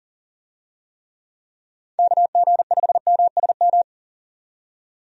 Morse code, heard as a keyed tone, about 40 wpm, 700 Hz. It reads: KG5MSM
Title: USA Call Signs - 6 Characters - 40wpm